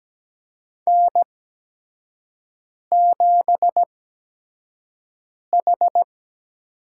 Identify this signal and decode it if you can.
Morse code, heard as a keyed tone, about 17 wpm, spaced as for 5 wpm, 700 Hz. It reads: N7H
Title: USA Call Signs - 5 Characters - 8x Character Spacing - 17wpm